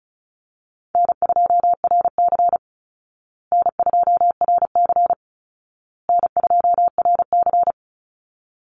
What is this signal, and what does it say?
Morse code, heard as a keyed tone, about 34 wpm, 700 Hz. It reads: N2RC N2RC N2RC
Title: Easy Call Signs Repeated 3x - 35wpm